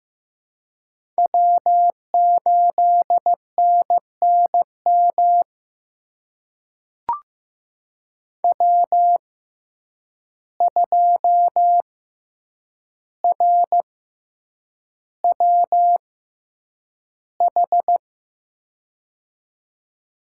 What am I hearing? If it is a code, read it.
Morse code, heard as a keyed tone, about 15 wpm, spaced as for 8 wpm, 700 Hz. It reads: W8NNM W2RWH
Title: USA Call Signs - 6x Character Spacing - 15wpm